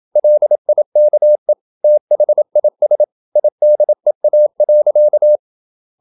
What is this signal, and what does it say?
Morse code, heard as a keyed tone, about 27 wpm, 600 Hz. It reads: LIKE THIS IDEA.
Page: Never 26wpm | CW med Gnister